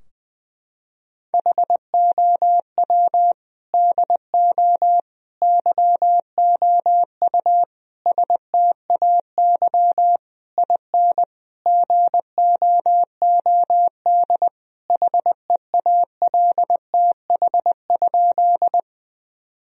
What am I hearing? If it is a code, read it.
Morse code, heard as a keyed tone, about 20 wpm, 700 Hz. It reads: HOW DO YOU STAY IN GOOD HEALTH?
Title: Sentences from Top 1000 Words - Review Part 3 - 20wpm